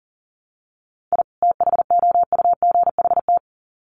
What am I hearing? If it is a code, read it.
Morse code, heard as a keyed tone, about 37 wpm, 700 Hz. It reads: I THOUGHT
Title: Sets of 2 Words 40wpm